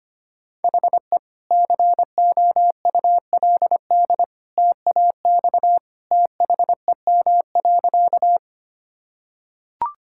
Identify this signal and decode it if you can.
Morse code, heard as a keyed tone, about 25 wpm, 700 Hz. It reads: HE COULD TAX THEM.
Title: Sentences from Top 1000 Words - Review Part 5 - 25wpm